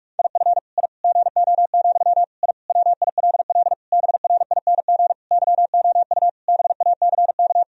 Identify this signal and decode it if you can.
Morse code, heard as a keyed tone, about 41 wpm, 700 Hz. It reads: IF I GO, I WILL BRING YOU BACK
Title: Sentences from Top 1000 Words - Review Part 3 - 45wpm